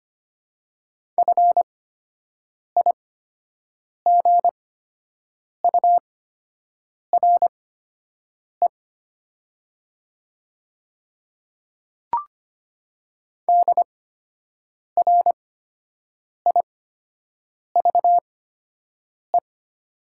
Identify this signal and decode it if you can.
Morse code, heard as a keyed tone, about 25 wpm, spaced as for 7 wpm, 690 Hz. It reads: FIGURE DRIVE
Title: 8x Character Spacing - Top 500 Words - 25wpm